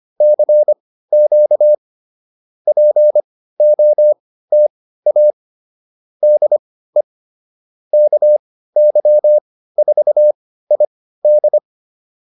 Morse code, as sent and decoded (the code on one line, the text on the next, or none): -.-. --.- / .--. --- - .- / -.. / -.- -.-- ....- .. -..
CQ POTA DE KY4ID